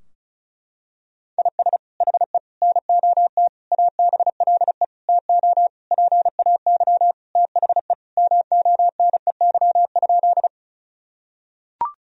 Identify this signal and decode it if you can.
Morse code, heard as a keyed tone, about 36 wpm, 700 Hz. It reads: IS HE NOT ABLE TO PAY THE MONEY?
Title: Sentences from Top 1000 Words - Review Part 2 - 35wpm